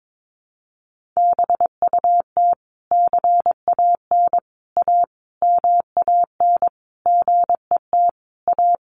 Morse code, heard as a keyed tone, about 22 wpm, 700 Hz. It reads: BUT CAN A MAN GET A
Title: Sentences from Top 1400 Words 22wpm